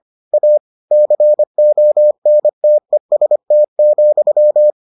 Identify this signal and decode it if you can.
Morse code, heard as a keyed tone, about 25 wpm, 600 Hz. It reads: A CONTEST,